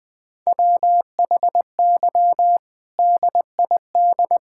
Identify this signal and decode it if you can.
Morse code, heard as a keyed tone, about 20 wpm, 700 Hz. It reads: WHY DID
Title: Sentences from Top 1600 Words 20wpm